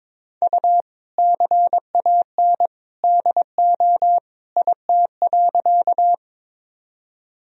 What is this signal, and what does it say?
Morse code, heard as a keyed tone, about 21 wpm, 700 Hz. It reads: U CAN DO IT.
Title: Sentences from Top 100 Words and Abbreviations 22wpm